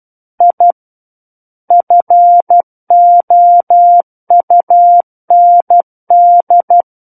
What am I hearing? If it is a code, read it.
Morse code, heard as a keyed tone, about 12 wpm, 700 Hz. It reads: I FOUND
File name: Book One / 03-Chapter 3 - 12 WPM